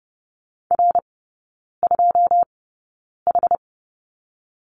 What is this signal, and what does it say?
Morse code, heard as a keyed tone, about 30 wpm, spaced as for 9 wpm, 700 Hz. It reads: R2H
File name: 081 Lesson 14e - Number 2 - Three Characters 30wpm